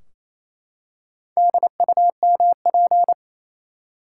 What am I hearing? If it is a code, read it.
Morse code, heard as a keyed tone, about 28 wpm, 700 Hz. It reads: DUMP
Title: Four Letter Words - 28wpm